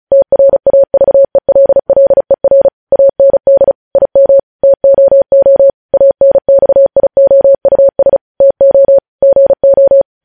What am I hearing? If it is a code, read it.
Morse code, heard as a keyed tone, about 36 wpm, 550 Hz. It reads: TRAVELLER AND IM TOO ANXIOUS TO GO